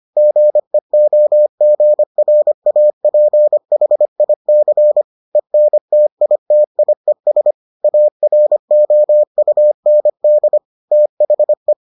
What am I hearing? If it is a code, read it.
Morse code, heard as a keyed tone, about 25 wpm, 600 Hz. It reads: GEOGRAPHIC ENTITIES AROUND THE